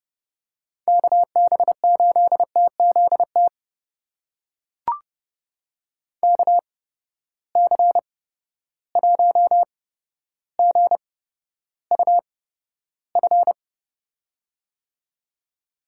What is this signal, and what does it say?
Morse code, heard as a keyed tone, about 30 wpm, spaced as for 13 wpm, 700 Hz. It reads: KB8TZT KC1GUF
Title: USA Call Signs - 6 Characters - 8x Character Spacing - 30wpm